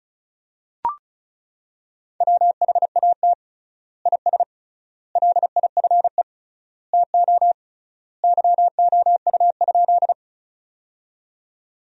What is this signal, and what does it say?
Morse code, heard as a keyed tone, about 36 wpm, 700 Hz. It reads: WHAT IS LIFE TO YOU?
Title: 3x Word-Spacing - Sentences from Top 300 Words - 35wpm